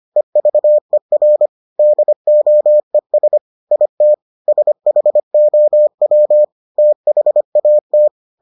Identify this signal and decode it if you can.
Morse code, heard as a keyed tone, about 25 wpm, 600 Hz. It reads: EVER DOES IT SHOW THAT